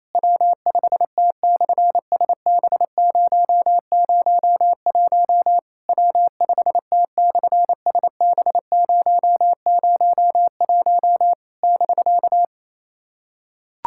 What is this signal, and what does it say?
Morse code, heard as a keyed tone, about 27 wpm, 700 Hz. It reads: W5T/SB001 W5T/SB001 <BK>
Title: SOTA QSO Practice 28wpm